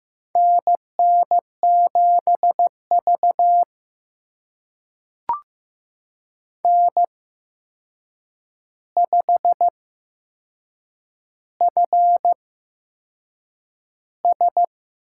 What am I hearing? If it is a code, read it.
Morse code, heard as a keyed tone, about 15 wpm, spaced as for 6 wpm, 700 Hz. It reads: NN7V N5FS
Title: USA Call Signs - 4 Characters - 8x Character Spacing - 15wpm